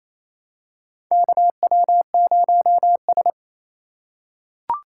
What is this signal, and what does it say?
Morse code, heard as a keyed tone, about 28 wpm, 700 Hz. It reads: KW0S
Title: USA Call Signs - 6x Character Spacing - 28wpm